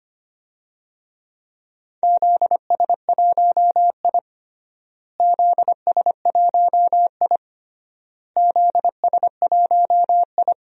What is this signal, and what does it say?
Morse code, heard as a keyed tone, about 25 wpm, 700 Hz. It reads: ZS1I ZS1I ZS1I
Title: DXCC-100 Call Signs - 4 Characters - Repeated 3x - 25wpm